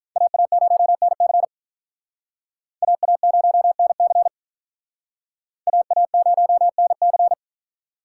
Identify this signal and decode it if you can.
Morse code, heard as a keyed tone, about 43 wpm, 700 Hz. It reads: AA0NC AA0NC AA0NC
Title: Speed Racing - USA Call Signs 35wpm